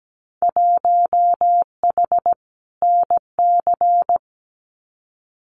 Morse code, heard as a keyed tone, about 17 wpm, 700 Hz. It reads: 1H NC
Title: Winter Field Day Exchanges 17wpm